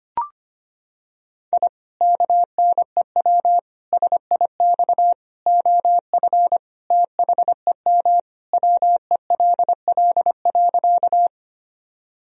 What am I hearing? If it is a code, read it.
Morse code, heard as a keyed tone, about 25 wpm, 700 Hz. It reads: I KNEW SIX OF THEM WELL.
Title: Sentences from Top 1000 Words - Review Part 2 - 25wpm